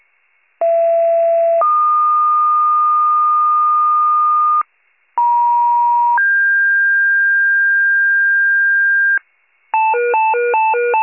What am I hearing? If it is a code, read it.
Two-tone page, heard as a one-second tone, then a three-second tone, about 670 then 1220 Hz, about 950 then 1600 Hz, with noise. It is heard from 0.6 to 9.2 s.